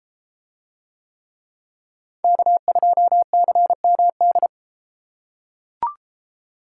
Morse code, heard as a keyed tone, about 33 wpm, 700 Hz. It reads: K2CMD